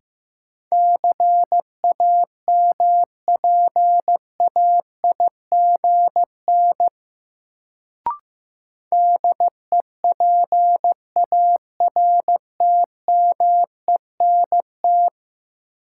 Morse code, heard as a keyed tone, about 15 wpm, 700 Hz. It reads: CAMPAIGN DEPARTMENT
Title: Sentences from Top 1200 Words - New Word Drill Alt - 15wpm